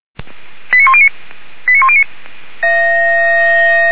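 Five-tone selective call (ZVEI), heard as quick runs of five tones, tones of about 70 ms. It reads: 89109 89109
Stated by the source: Five-tone ZVEI selective call 89133 received by SDR, logged by Feuerwehr Orlamünde.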